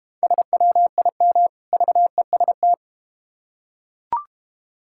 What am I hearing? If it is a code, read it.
Morse code, heard as a keyed tone, about 31 wpm, 700 Hz. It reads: SWIM VEST
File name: practice / 25wpm / Difficult Words - Speed-Racing with Hints - 25wpm